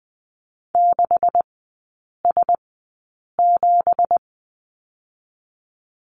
Morse code, heard as a keyed tone, about 20 wpm, spaced as for 8 wpm, 700 Hz. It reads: 6S7
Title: 178 Lesson 29e - Slash - Three Characters 20wpm